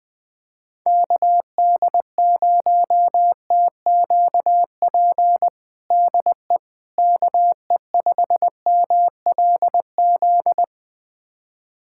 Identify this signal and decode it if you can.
Morse code, heard as a keyed tone, about 21 wpm, 700 Hz. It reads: KD0TQP DE KE5MLZ
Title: USA Callsign DE Callsign - 6 Characters - 20wpm